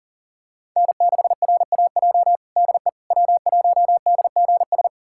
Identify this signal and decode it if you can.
Morse code, heard as a keyed tone, about 37 wpm, 700 Hz. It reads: N6RAJ DE W1DGS
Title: USA Callsign DE Callsign - 5 Characters - 40wpm